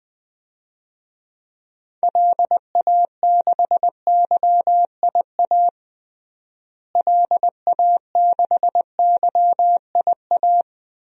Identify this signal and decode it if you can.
Morse code, heard as a keyed tone, about 20 wpm, 700 Hz. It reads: LA6YIA LA6YIA
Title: DXCC-100 Call Signs - 6 Characters - Repeated 3x - 20wpm